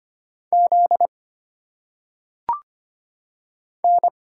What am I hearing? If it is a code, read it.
Morse code, heard as a keyed tone, about 25 wpm, 700 Hz. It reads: Z N
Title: Single Letters - Phonetic Pronunciation 25wpm